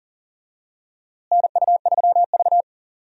Morse code, heard as a keyed tone, about 40 wpm, 700 Hz. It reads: NU3V
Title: USA Call Signs - 4 Characters - Repeated 3x - 40wpm